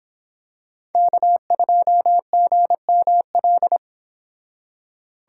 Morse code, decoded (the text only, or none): K2GML